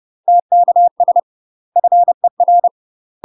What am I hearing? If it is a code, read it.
Morse code, heard as a keyed tone, about 30 wpm, 700 Hz. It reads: TKS FER
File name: UK MASTER QSOs TRAINING / 30WPM / UK MASTER QSOs 0031